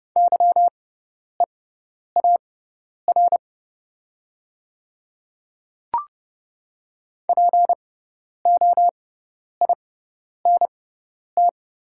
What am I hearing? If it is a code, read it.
Morse code, heard as a keyed tone, about 31 wpm, spaced as for 10 wpm, 700 Hz. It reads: YEAR POINT
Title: 6x Character Spacing - Top 500 Words - 30wpm